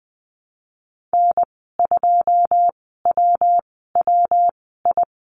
Morse code, heard as a keyed tone, about 20 wpm, spaced as for 15 wpm, 700 Hz. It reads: N2WWI
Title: 147 Lesson 24g - Letter y - Call Signs 20wpm